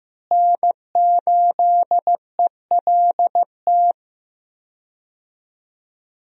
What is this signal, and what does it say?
Morse code, heard as a keyed tone, about 14 wpm, 700 Hz. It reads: N8ELT
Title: USA Call Signs - 5 Characters - Repeated 3x - 15wpm